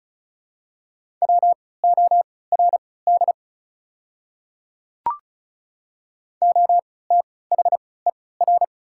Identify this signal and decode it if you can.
Morse code, heard as a keyed tone, about 36 wpm, spaced as for 20 wpm, 690 Hz. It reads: WORD OTHER
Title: 3x Character Spacing - Top 100 Words - 35wpm